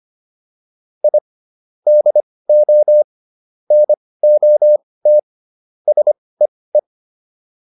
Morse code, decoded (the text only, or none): I DO NOT SEE